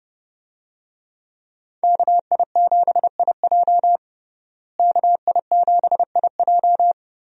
Morse code, heard as a keyed tone, about 30 wpm, 700 Hz. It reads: KI7IJ KI7IJ
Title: USA Call Signs - 5 Characters - Repeated 3x - 30wpm